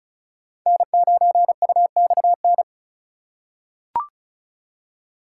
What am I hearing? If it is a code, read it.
Morse code, heard as a keyed tone, about 34 wpm, 700 Hz. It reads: N9UXN
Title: USA Call Signs - 5 Characters - 4x Character Spacing - 35wpm